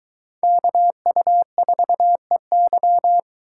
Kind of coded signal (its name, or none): Morse code